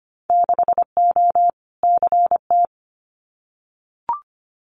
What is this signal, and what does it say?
Morse code, heard as a keyed tone, about 25 wpm, 700 Hz. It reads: 6O CT
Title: Winter Field Day Exchanges Encore 25wpm